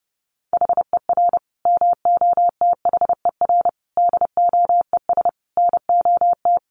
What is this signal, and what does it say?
Morse code, heard as a keyed tone, about 30 wpm, 700 Hz. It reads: HER MOTHER DOES NOT
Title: Sentences from Top 1400 Words 30wpm